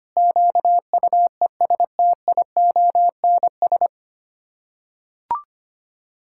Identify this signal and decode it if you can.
Morse code, heard as a keyed tone, about 25 wpm, 700 Hz. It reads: QUESTIONS
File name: Sets of 9 Words - 25wpm